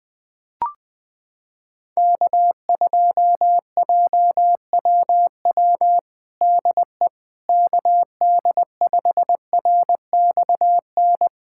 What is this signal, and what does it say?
Morse code, heard as a keyed tone, about 20 wpm, 700 Hz. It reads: K2JWW DE KD5RXN
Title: USA Callsign DE Callsign 20wpm